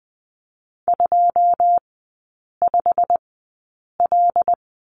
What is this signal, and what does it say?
Morse code, heard as a keyed tone, about 20 wpm, spaced as for 8 wpm, 700 Hz. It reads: 25L
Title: 100 Lesson 17e - Letter u - Three Characters 20wpm